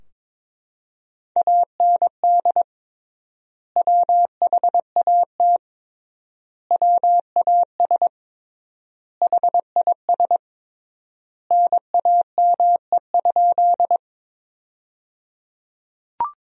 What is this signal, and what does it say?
Morse code, heard as a keyed tone, about 22 wpm, 700 Hz. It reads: AND WHAT WAS HIS NAME?
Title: 3x Word-Spacing - Sentences from Top 200 Words - 22wpm